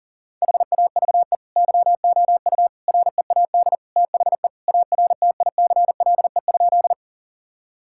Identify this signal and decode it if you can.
Morse code, heard as a keyed tone, about 40 wpm, 700 Hz. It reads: HAVE YOU READ THE ARTICLE?
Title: Sentences from Top 1000 Words - Review Part 5 - 40wpm